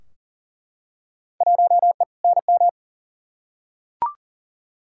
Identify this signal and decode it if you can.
Morse code, heard as a keyed tone, about 40 wpm, 700 Hz. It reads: 1E NM
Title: Field Day Exchanges - 40wpm